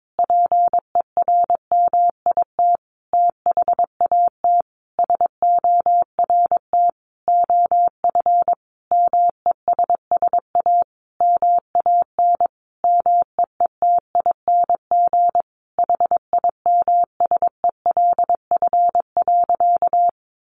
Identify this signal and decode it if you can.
Morse code, heard as a keyed tone, about 22 wpm, 700 Hz. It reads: PERMIT THAT SORT OF MESS A MAN MEETING HIMSELF.
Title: A Sound Of Thunder 22wpm